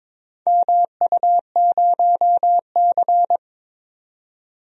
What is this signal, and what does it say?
Morse code, heard as a keyed tone, about 21 wpm, 700 Hz. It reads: MU0C
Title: DXCC-100 Call Signs - 4 Characters - 22wpm